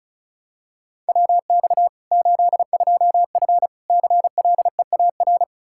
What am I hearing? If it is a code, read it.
Morse code, heard as a keyed tone, about 36 wpm, 700 Hz. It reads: WX 82F CLEAR
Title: Practice QSO - Exchange 2 - Info - 35wpm